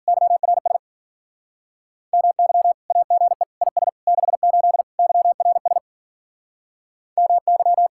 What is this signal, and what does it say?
Morse code, heard as a keyed tone, about 52 wpm, 700 Hz. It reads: YRS MY AGE IS 68 YRS MY